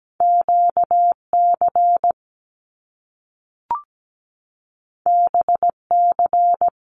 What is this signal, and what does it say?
Morse code, heard as a keyed tone, about 17 wpm, 700 Hz. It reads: QC BC
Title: Canadian Provinces and Territories 17wpm